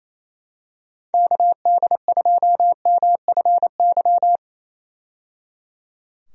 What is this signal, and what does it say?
Morse code, heard as a keyed tone, about 28 wpm, 700 Hz. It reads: KD2MFY
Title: USA Call Signs - 6 Characters - 28wpm